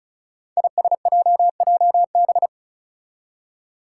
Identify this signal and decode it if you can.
Morse code, heard as a keyed tone, about 36 wpm, 700 Hz. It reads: ISJJB
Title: Five Challenging Letter-Number Combinations 35wpm